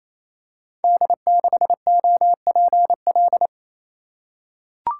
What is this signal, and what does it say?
Morse code, heard as a keyed tone, about 27 wpm, 700 Hz. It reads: D6OPL
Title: Code Groups - 5 Letter:Numbers - 28wpm